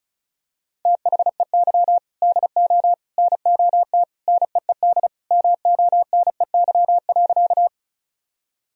Morse code, heard as a keyed tone, about 36 wpm, 700 Hz. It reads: THEY DO NOT NEED MONEY.
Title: Sentences from Top 1000 Words - Review Part 1 - 35wpm